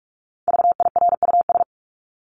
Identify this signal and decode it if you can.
Morse code, heard as a keyed tone, about 44 wpm, 700 Hz. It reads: VIRUS
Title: Top 1251-1500 Words 45wpm